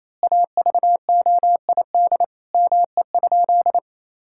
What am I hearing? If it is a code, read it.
Morse code, heard as a keyed tone, about 28 wpm, 700 Hz. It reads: AVOID ME?
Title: Sentences from Top 1000 Words - Review Part 6 - 28wpm